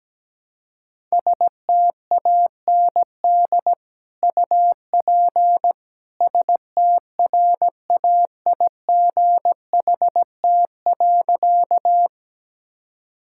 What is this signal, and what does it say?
Morse code, heard as a keyed tone, about 17 wpm, 700 Hz. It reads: STAND UP STRAIGHT.
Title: Sentences from Top 1000 Words - Review Part 6 - 17wpm